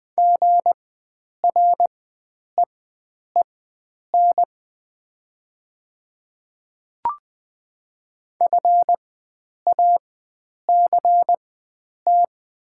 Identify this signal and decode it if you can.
Morse code, heard as a keyed tone, about 20 wpm, spaced as for 9 wpm, 700 Hz. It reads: GREEN FACT
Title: 4x Character Spacing - Top 500 Words - 20wpm